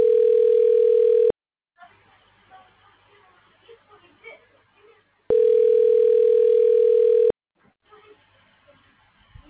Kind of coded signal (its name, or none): telephone call progress tone